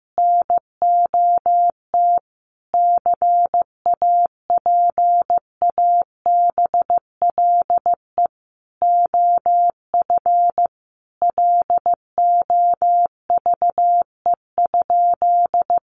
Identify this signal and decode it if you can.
Morse code, heard as a keyed tone, about 15 wpm, 700 Hz. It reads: NOT CAPABLE OF LOVE?
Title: Sentences from Top 1200 Words 15wpm